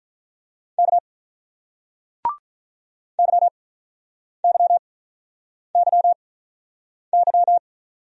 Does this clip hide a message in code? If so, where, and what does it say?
Yes: Morse code from 0.8 to 7.6 s, heard as a keyed tone, about 48 wpm, spaced as for 9 wpm, 700 Hz. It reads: K YYYY